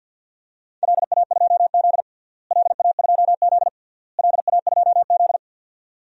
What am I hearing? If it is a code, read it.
Morse code, heard as a keyed tone, about 50 wpm, 700 Hz. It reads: LA2Z LA2Z LA2Z
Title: DXCC-100 Call Signs - 4 Characters - Repeated 3x - 50wpm